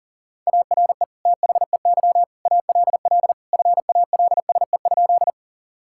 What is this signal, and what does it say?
Morse code, heard as a keyed tone, about 40 wpm, 700 Hz. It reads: ARE THEY ALL FALSE?
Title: Sentences from Top 1600 Words 40wpm